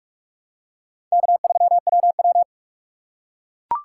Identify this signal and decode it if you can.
Morse code, heard as a keyed tone, about 44 wpm, 700 Hz. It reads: K3WW
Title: USA Call Signs - 2x Character Spacing - 45wpm